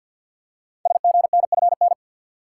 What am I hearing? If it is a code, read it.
Morse code, heard as a keyed tone, about 48 wpm, 700 Hz. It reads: SGNFN
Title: Code Groups - 5 Letters - 50wpm